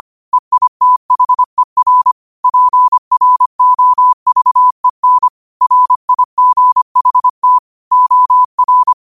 Morse code, heard as a keyed tone, about 25 wpm, 1000 Hz. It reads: EITHER PROVEN RIGHT OR
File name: qotdfiles / 25wpm